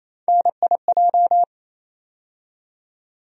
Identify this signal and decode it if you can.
Morse code, heard as a keyed tone, about 28 wpm, 700 Hz. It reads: NIJ
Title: USA Call Sign Postfixes - 3 Letters 28wpm